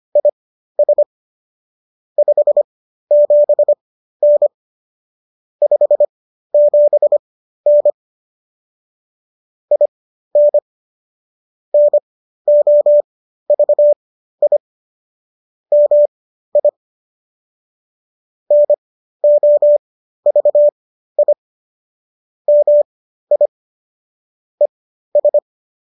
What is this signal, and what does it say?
Morse code, heard as a keyed tone, about 25 wpm, spaced as for 13 wpm, 590 Hz. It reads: IS 57N 57N IN NOVI MI NOVI MI ES